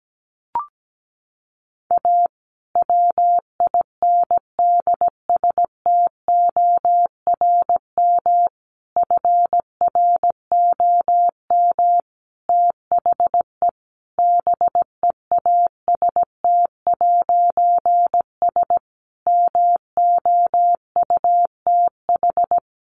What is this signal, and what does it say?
Morse code, heard as a keyed tone, about 17 wpm, 700 Hz. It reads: A WINDSTORM FROM THE BEAST'S MOUTH
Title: A Sound Of Thunder 17wpm